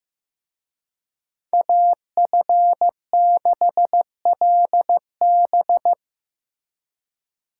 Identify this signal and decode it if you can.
Morse code, heard as a keyed tone, about 18 wpm, 700 Hz. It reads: AF6LB
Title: Speed Racing - USA 2x2 Call Signs 15wpm